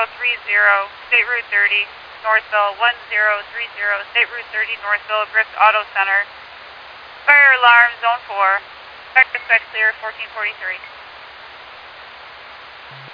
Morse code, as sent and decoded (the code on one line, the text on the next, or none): none